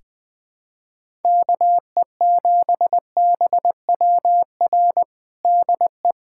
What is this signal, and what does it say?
Morse code, heard as a keyed tone, about 21 wpm, 700 Hz. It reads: KE7BWR DE
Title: USA Callsign DE Callsign - 6 Characters - 20wpm